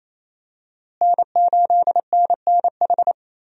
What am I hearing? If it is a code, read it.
Morse code, heard as a keyed tone, about 28 wpm, 700 Hz. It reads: N8NNH
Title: USA Call Signs - 5 Characters - Repeated 2x - 28wpm